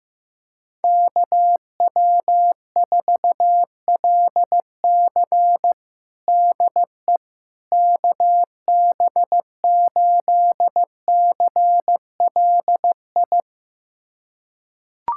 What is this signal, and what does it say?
Morse code, heard as a keyed tone, about 15 wpm, 700 Hz. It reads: KW4LC DE KB8CLI
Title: USA Callsign DE Callsign 15wpm